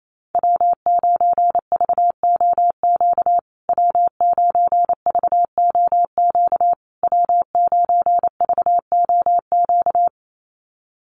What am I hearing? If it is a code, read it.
Morse code, heard as a keyed tone, about 28 wpm, 700 Hz. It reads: W9VOQ W9VOQ W9VOQ
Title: USA Call Signs Repeated 3x 28wpm